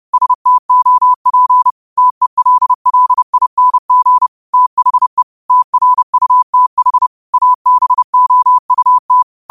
Morse code, heard as a keyed tone, about 30 wpm, 1000 Hz. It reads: STOP TELLING THE TRUTH ABOUT
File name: qotdfiles / 30wpm